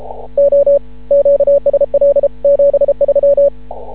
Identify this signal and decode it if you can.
Morse code, heard as a keyed tone, about 33 wpm, 560 Hz. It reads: O QSL 73
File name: CW / ha6oi 2